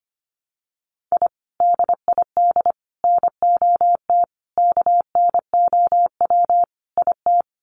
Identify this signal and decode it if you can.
Morse code, heard as a keyed tone, about 25 wpm, 700 Hz. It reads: I DID NOT KNOW IT
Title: Sets of 5 Words Encore2 25wpm